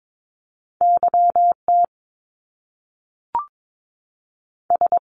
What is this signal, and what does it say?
Morse code, heard as a keyed tone, about 22 wpm, 700 Hz. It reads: YT S
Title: Top Two Letter Combinations in Words 22wpm